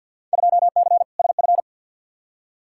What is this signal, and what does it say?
Morse code, heard as a keyed tone, about 52 wpm, 700 Hz. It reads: 2C SF
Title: Field Day Exchanges Encore - 50wpm